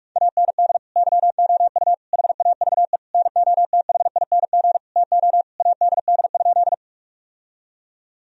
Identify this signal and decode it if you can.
Morse code, heard as a keyed tone, about 48 wpm, 700 Hz. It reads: AND YOU HAVE NOTHING TO ADD?
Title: Sentences from Top 1000 Words - Review Part 1 - 45wpm